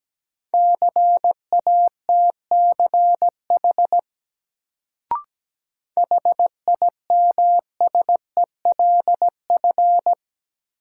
Morse code, heard as a keyed tone, about 17 wpm, 700 Hz. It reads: CATCH HIMSELF
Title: Sentences from Top 1500 Words - New Word Drill - 17wpm